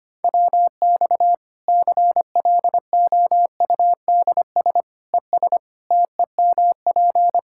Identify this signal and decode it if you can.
Morse code, heard as a keyed tone, about 25 wpm, 700 Hz. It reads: WX CLOUDS ES TEMP
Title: Practice QSO - Exchange 2 - Info - 25wpm